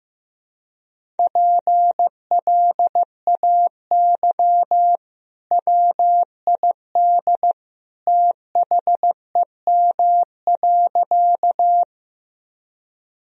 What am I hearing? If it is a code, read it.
Morse code, heard as a keyed tone, about 15 wpm, 700 Hz. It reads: PLAY WID THEM.
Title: Sentences from Top 200 Words and Abbreviations 15wpm